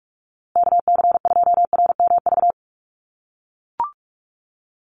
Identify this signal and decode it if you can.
Morse code, heard as a keyed tone, about 44 wpm, 700 Hz. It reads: KC2RMV
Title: USA Call Signs Repeated 3x 45wpm